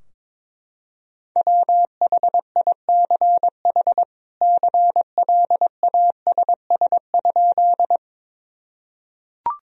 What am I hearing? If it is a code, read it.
Morse code, heard as a keyed tone, about 21 wpm, 700 Hz. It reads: WHICH CLASS?
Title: Sentences from Top 1000 Words - Review Part 2 - 22wpm